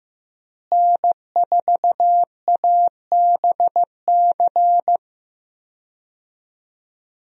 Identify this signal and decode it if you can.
Morse code, heard as a keyed tone, about 15 wpm, 700 Hz. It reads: N4ABC
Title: USA Call Signs - 5 Characters - Repeated 3x - 15wpm